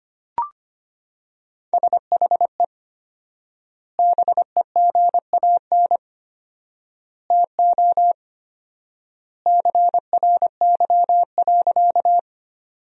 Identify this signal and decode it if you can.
Morse code, heard as a keyed tone, about 25 wpm, 700 Hz. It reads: SHE BEGAN TO CRY.
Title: 4x Word-Spacing - Sentences from Top 500 Words - 25wpm